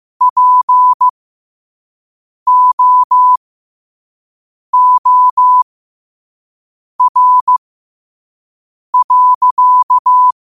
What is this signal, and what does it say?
Morse code, heard as a keyed tone, about 15 wpm, spaced as for 5 wpm, 1000 Hz. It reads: POOR.
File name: qotdfiles / 5wpm